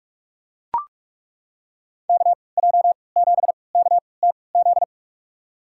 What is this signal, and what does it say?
Morse code, heard as a keyed tone, about 43 wpm, spaced as for 25 wpm, 700 Hz. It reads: KJ7KTZ
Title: USA Call Signs - 6 Characters - 3x Character Spacing - 45wpm